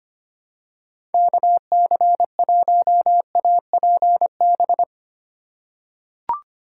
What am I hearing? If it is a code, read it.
Morse code, heard as a keyed tone, about 25 wpm, 700 Hz. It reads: KC1APB
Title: USA Call Signs - 2x Character Spacing - 25wpm